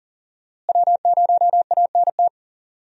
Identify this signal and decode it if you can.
Morse code, heard as a keyed tone, about 39 wpm, 700 Hz. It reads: W0ANT
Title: USA Call Signs - 5 Characters - Repeated 2x - 40wpm